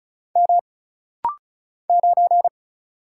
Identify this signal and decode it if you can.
Morse code, heard as a keyed tone, about 36 wpm, 700 Hz. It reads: M 9
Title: Single Letter-Number - Rapid-Fire 35wpm